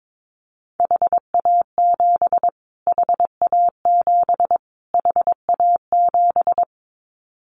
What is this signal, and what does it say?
Morse code, heard as a keyed tone, about 22 wpm, 700 Hz. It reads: HA7 HA7 HA7
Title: Callsign Prefixes for Top 25 DXCC - Repeated 3x - 22wpm